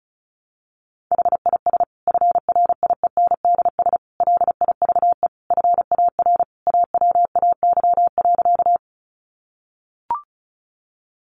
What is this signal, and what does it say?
Morse code, heard as a keyed tone, about 34 wpm, 700 Hz. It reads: HIS FRIENDS LIVE FAR AWAY.
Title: Additional Easy Phrases 35wpm